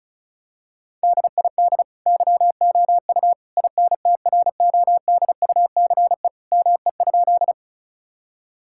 Morse code, heard as a keyed tone, about 34 wpm, 700 Hz. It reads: DID YOU INTRODUCE ME?
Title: Sentences from Top 1600 Words 35wpm